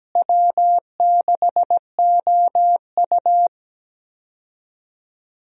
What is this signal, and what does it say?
Morse code, heard as a keyed tone, about 17 wpm, 700 Hz. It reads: W6OU
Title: USA Call Signs - 4 Characters - Repeated 3x - 17wpm